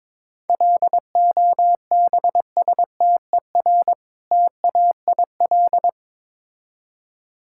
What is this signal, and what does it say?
Morse code, heard as a keyed tone, about 22 wpm, 700 Hz. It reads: LOBSTER TAIL
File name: Difficult Words with Hints - 22wpm